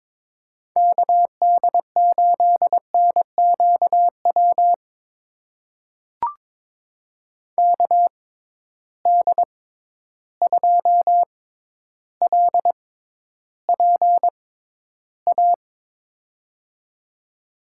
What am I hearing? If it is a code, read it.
Morse code, heard as a keyed tone, about 22 wpm, spaced as for 11 wpm, 700 Hz. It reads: KD8NQW KD2LPA
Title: USA Call Signs - 6 Characters - 6x Character Spacing - 22wpm